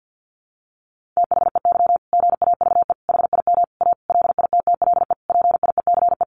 Morse code, heard as a keyed tone, about 50 wpm, 700 Hz. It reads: THEY GAVE HIM A LITTLE PIECE
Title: Sentences from Top 1500 Words 50wpm